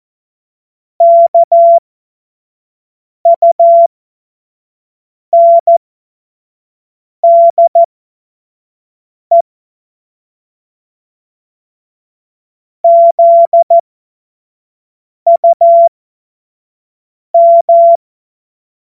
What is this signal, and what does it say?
Morse code, heard as a keyed tone, about 14 wpm, spaced as for 5 wpm, 680 Hz. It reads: KUNDE ZUM
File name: Morsekurs / Witze / Witz 11. April 2023 25 F70